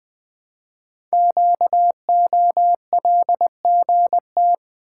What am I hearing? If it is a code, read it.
Morse code, heard as a keyed tone, about 20 wpm, 700 Hz. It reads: QOLGT
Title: Code Groups - 5 Letters - 20wpm